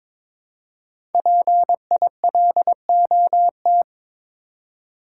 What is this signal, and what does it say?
Morse code, heard as a keyed tone, about 22 wpm, 700 Hz. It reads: PILOT
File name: 5 Letter Words - First 3 Letters Given - 22wpm